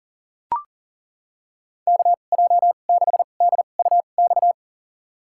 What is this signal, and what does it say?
Morse code, heard as a keyed tone, about 40 wpm, spaced as for 29 wpm, 700 Hz. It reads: KJ6DUX
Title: USA Call Signs - 6 Characters - 2x Character Spacing - 40wpm